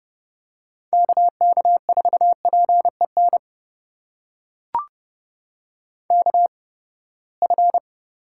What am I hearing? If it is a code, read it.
Morse code, heard as a keyed tone, about 32 wpm, spaced as for 21 wpm, 700 Hz. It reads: KK4PEN KF
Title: USA Call Signs - 6 Characters - 8x Character Spacing - 30wpm